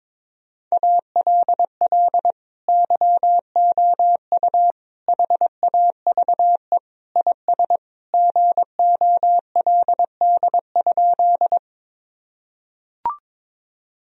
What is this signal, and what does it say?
Morse code, heard as a keyed tone, about 22 wpm, 700 Hz. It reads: ALL YOU HAVE IS GOLD?
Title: Sentences from Top 1000 Words - Review Part 3 - 22wpm